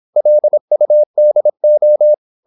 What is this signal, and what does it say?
Morse code, heard as a keyed tone, about 26 wpm, 600 Hz. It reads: LUDO
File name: Korte ord DK 26wpm